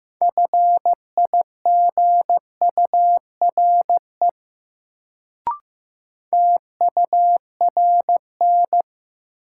Morse code, heard as a keyed tone, about 15 wpm, 700 Hz. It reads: FIGURE TURN
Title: Sets of 4 Words - New Word Drill - 15wpm